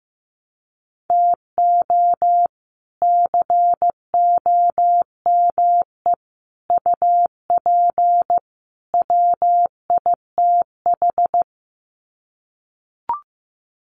Morse code, heard as a keyed tone, about 15 wpm, 700 Hz. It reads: TO COME UP WITH
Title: Sets of 4 Words 15wpm